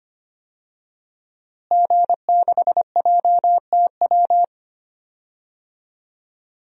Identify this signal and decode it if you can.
Morse code, heard as a keyed tone, about 25 wpm, 700 Hz. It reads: G6JTW
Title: DXCC-100 Call Signs - 5 Characters - Repeated 3x - 25wpm